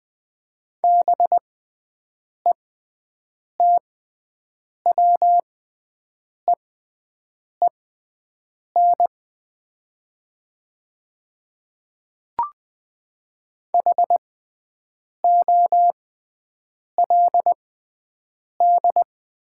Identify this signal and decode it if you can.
Morse code, heard as a keyed tone, about 20 wpm, spaced as for 7 wpm, 700 Hz. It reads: BETWEEN HOLD
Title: 6x Character Spacing - Top 500 Words - 20wpm